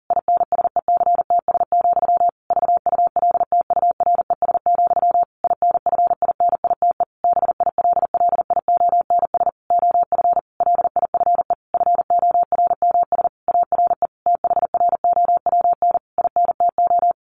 Morse code, heard as a keyed tone, about 40 wpm, 700 Hz. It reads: INSECTS, VULTURES, INFINITE BILLIONS OF LIFE FORMS ARE THROWN INTO
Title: A Sound Of Thunder 40wpm